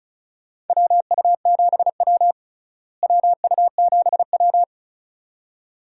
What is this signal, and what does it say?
Morse code, heard as a keyed tone, about 34 wpm, 700 Hz. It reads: WU7W WU7W
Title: USA Call Signs - 4 Characters - Repeated 2x - 35wpm